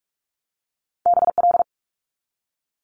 Morse code, heard as a keyed tone, about 46 wpm, 700 Hz. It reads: BL
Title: Multi-letter Phonemes 45wpm